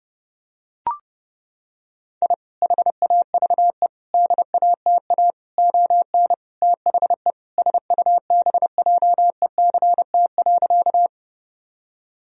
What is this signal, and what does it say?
Morse code, heard as a keyed tone, about 32 wpm, 700 Hz. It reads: I HAVE DATA ON THE SUBJECT.
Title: Sentences from Top 1000 Words - Review Part 3 - 30wpm